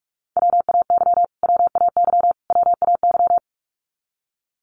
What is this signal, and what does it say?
Morse code, heard as a keyed tone, about 44 wpm, 700 Hz. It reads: WAY WAY WAY
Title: Top 200 Words Repeated 45wpm